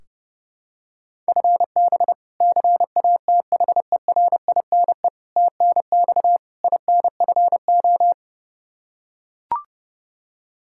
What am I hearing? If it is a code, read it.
Morse code, heard as a keyed tone, about 30 wpm, 700 Hz. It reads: FB CATHERINE TNX INFO
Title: K1USN SST QSO Practice 30wpm